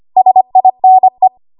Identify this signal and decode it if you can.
Morse code, heard as a keyed tone, about 25 wpm, 750 Hz. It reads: SINE
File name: Archive / 25 WPM / 25WPM